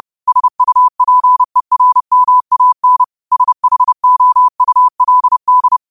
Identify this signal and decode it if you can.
Morse code, heard as a keyed tone, about 30 wpm, 1000 Hz. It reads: SUPERMAN SHOULD